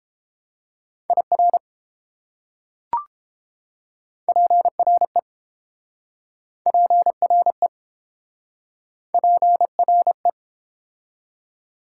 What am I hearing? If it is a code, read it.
Morse code, heard as a keyed tone, about 31 wpm, 700 Hz. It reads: IR PRE PRE PRE